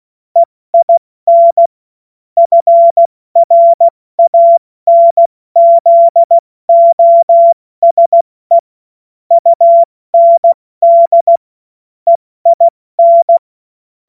Morse code, heard as a keyed tone, about 16 wpm, spaced as for 14 wpm, 680 Hz. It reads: EIN FRANZOSE UND EIN
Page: cw-witze